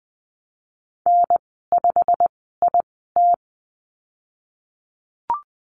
Morse code, heard as a keyed tone, about 20 wpm, spaced as for 14 wpm, 700 Hz. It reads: N5IT
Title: 088 Lesson 15g - Number 5 - Call Signs 20wpm